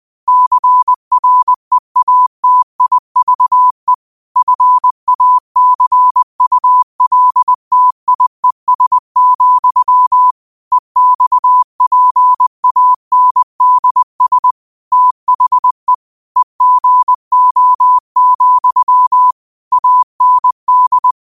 Morse code, heard as a keyed tone, about 20 wpm, 1000 Hz. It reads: CREATIVE FACULTIES, EXPANDS THE EGO, AND
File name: qotdfiles / 20wpm